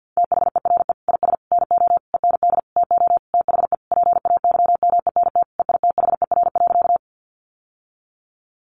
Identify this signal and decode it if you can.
Morse code, heard as a keyed tone, about 50 wpm, 700 Hz. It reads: THERE IS NO END TO THE PAYMENT EITHER.
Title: Sentences from Top 1200 Words 50wpm